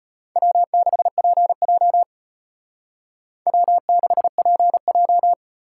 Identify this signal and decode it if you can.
Morse code, heard as a keyed tone, about 32 wpm, 700 Hz. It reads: W6PJ W6PJ